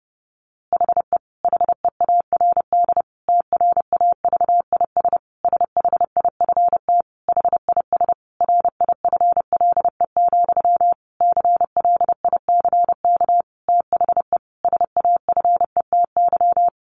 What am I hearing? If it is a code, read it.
Morse code, heard as a keyed tone, about 28 wpm, 700 Hz. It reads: HE HEARD TRAVIS SHIFT HIS RIFLE, CLICK THE SAFETY
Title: A Sound Of Thunder 30wpm